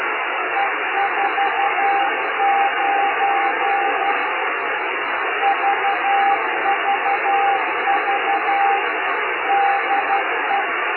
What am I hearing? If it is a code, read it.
Morse code, heard as a keyed tone, about 12 wpm, 800 Hz, with radio noise. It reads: E49 VVV DE
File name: Beacon / FujisanPJ / 8J1FUJ 2022